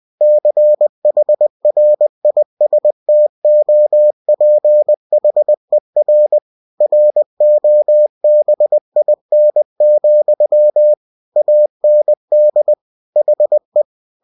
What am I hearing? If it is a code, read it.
Morse code, heard as a keyed tone, about 20 wpm, 600 Hz. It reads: CHRISTOPHER ROBIN, AND HE
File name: cw-books / winnie-the-pooh / winnie-20wpm-page-0062